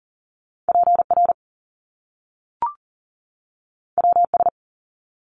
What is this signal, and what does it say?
Morse code, heard as a keyed tone, about 38 wpm, 700 Hz. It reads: PR WS
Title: Top Two Letter Combinations in Words 40wpm